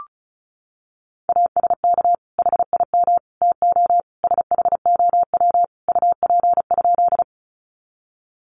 Morse code, heard as a keyed tone, about 34 wpm, 700 Hz. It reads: ASK HIM TO SHOW UP?
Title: Sentences from Top 200 Words 35wpm